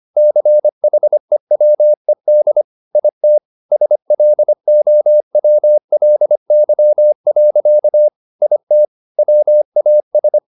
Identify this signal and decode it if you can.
Morse code, heard as a keyed tone, about 25 wpm, 600 Hz. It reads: CHEWED IT SLOWLY. IT WAS
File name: the old man and the sea part 1 25wpm